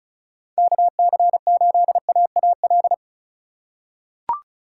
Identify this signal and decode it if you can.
Morse code, heard as a keyed tone, about 36 wpm, 700 Hz. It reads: KC8AAL
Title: USA Call Signs - 6 Characters - 2x Character Spacing - 35wpm